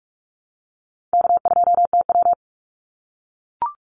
Morse code, heard as a keyed tone, about 44 wpm, 700 Hz. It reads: K2TW
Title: Easy Call Signs - 45wpm